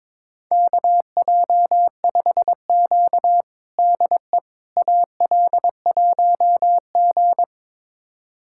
Morse code, heard as a keyed tone, about 22 wpm, 700 Hz. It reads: KJ5Q DE AL1G
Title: USA Callsign DE Callsign - 4 Characters - 22wpm